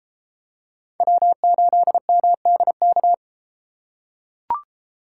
Morse code, heard as a keyed tone, about 33 wpm, 700 Hz. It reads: W8MDK